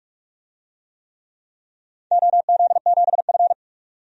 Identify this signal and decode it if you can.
Morse code, heard as a keyed tone, about 44 wpm, 690 Hz. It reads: OZ7F
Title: DXCC-100 Call Signs - 4 Characters - 45wpm